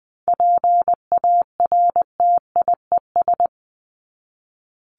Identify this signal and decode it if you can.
Morse code, heard as a keyed tone, about 20 wpm, 700 Hz. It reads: PARTIES
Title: ES:S Suffix - Top 100 Words with Base Words 20wpm